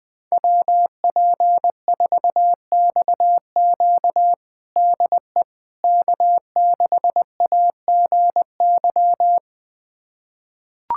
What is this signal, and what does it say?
Morse code, heard as a keyed tone, about 20 wpm, 700 Hz. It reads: WP4XQ DE K6AGY
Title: USA Callsign DE Callsign 20wpm